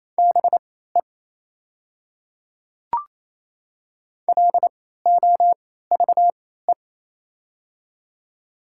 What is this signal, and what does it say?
Morse code, heard as a keyed tone, about 28 wpm, spaced as for 16 wpm, 700 Hz. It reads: BE LOVE
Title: 3x Character Spacing - Top 500 Words - 28wpm